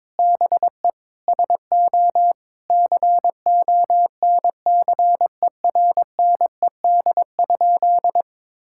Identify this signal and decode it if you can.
Morse code, heard as a keyed tone, about 22 wpm, 700 Hz. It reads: BE SO CONCERNED?
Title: Sentences from Top 1000 Words - Review Part 6 - 22wpm